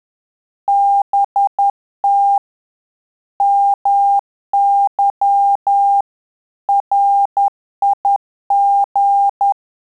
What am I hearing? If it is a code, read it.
Morse code, heard as a keyed tone, about 10 wpm, 800 Hz. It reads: BT MY RIG
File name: Telegraphy-group / QSO CW 50 CPM